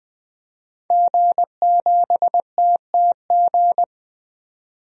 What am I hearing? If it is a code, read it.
Morse code, heard as a keyed tone, about 21 wpm, 700 Hz. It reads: G7TTG
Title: DXCC-100 Call Signs - 5 Characters - 20wpm